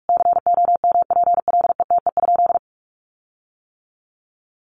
Morse code, heard as a keyed tone, about 44 wpm, 700 Hz. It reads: COMPLETE?
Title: Sentences from Top 1500 Words 45wpm